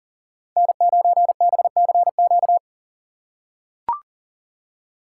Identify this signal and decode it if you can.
Morse code, heard as a keyed tone, about 38 wpm, 700 Hz. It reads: N9BCQ
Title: USA Call Signs - 5 Characters - 3x Character Spacing - 40wpm